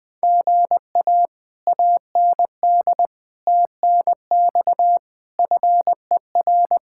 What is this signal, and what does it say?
Morse code, heard as a keyed tone, about 20 wpm, 700 Hz. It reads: GA AND TNX FER
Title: Speed-Racing Common QSO Phrases Encore - 17wpm